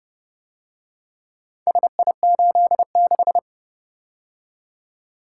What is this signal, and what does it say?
Morse code, heard as a keyed tone, about 30 wpm, 700 Hz. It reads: SI86